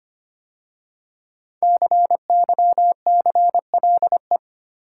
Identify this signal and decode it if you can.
Morse code, heard as a keyed tone, about 25 wpm, 700 Hz. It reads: CYCLE
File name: 5 Letter Words - First 3 Letters Given - 25wpm